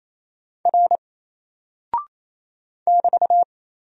Morse code, heard as a keyed tone, about 27 wpm, 700 Hz. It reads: R <BT>
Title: Single Character (Letter:Number:Punctuation:Prosign) Alt 28wpm